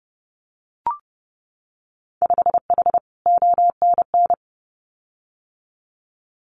Morse code, heard as a keyed tone, about 30 wpm, 700 Hz. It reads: 5H ONN
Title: Winter Field Day Exchanges Encore 30wpm